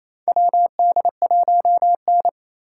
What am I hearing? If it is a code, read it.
Morse code, heard as a keyed tone, about 28 wpm, 700 Hz. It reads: WD1N
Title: USA Call Signs - 4 Characters - 3x Character Spacing - 28wpm